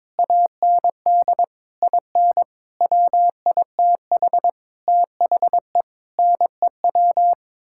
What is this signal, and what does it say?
Morse code, heard as a keyed tone, about 22 wpm, 700 Hz. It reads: AND IN WITH THE NEW
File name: Sets of 9 Words - 22wpm